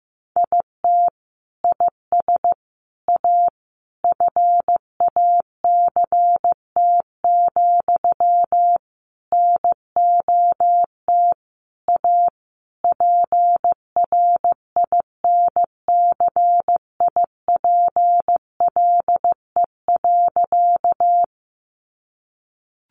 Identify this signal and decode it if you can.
Morse code, heard as a keyed tone, about 15 wpm, 700 Hz. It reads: IT IS A FACT, NOT A PRINCIPLE.
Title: Sentences from Top 1400 Words 15wpm